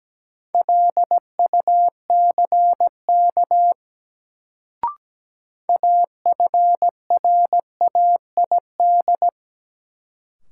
Morse code, heard as a keyed tone, about 17 wpm, 700 Hz. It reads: LUCK AFRAID
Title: Sentences from Top 1500 Words - New Word Drill Alt - 17wpm